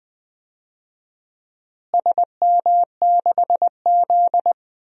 Morse code, heard as a keyed tone, about 20 wpm, 700 Hz. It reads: SM6Z
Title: DXCC-100 Call Signs - 4 Characters - Repeated 2x - 20wpm